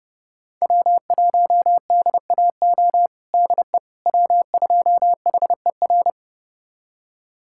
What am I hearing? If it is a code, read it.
Morse code, heard as a keyed tone, about 30 wpm, 700 Hz. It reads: W1DAO DE W2HER
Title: USA Callsign DE Callsign - 5 Characters - 30wpm